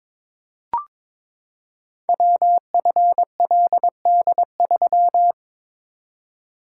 Morse code, heard as a keyed tone, about 22 wpm, 700 Hz. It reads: WFLD3
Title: Code Groups - 5 Letter:Numbers - 22wpm